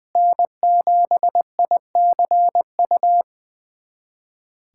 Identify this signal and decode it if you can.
Morse code, heard as a keyed tone, about 21 wpm, 700 Hz. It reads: N7ICU
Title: USA Callsign DE Callsign 20wpm